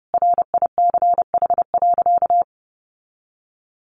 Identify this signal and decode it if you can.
Morse code, heard as a keyed tone, about 30 wpm, 700 Hz. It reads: RICH.
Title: Sentences from Top 1100 Words 30wpm